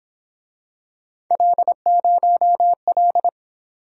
Morse code, heard as a keyed tone, about 26 wpm, 700 Hz. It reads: L0L
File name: Speed-Racing - Three Letter-Number Combinations - 22wpm